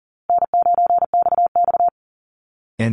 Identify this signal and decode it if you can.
Morse code, heard as a keyed tone, about 40 wpm, 700 Hz. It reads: N9XX
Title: Call Signs Repeated 2x - 40wpm